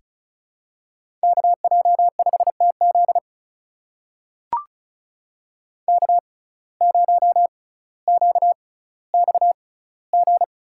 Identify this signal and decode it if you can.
Morse code, heard as a keyed tone, about 36 wpm, spaced as for 18 wpm, 700 Hz. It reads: KJ5TZ K0QXG
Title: USA Call Signs - 5 Characters - 6x Character Spacing - 35wpm